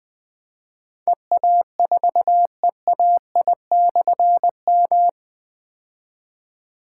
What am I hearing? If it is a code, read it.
Morse code, heard as a keyed tone, about 20 wpm, 700 Hz. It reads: EA4EAI/M
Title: Difficult Call Signs 20wpm